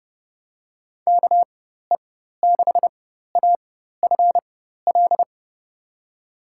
Morse code, heard as a keyed tone, about 30 wpm, spaced as for 14 wpm, 700 Hz. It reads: KE6AFL
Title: USA Call Signs - 6 Characters - 4x Character Spacing - 30wpm